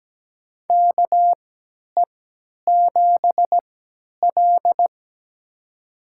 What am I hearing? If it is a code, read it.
Morse code, heard as a keyed tone, about 17 wpm, spaced as for 10 wpm, 700 Hz. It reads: KE7L
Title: USA Call Signs - 3x Character Spacing - 17wpm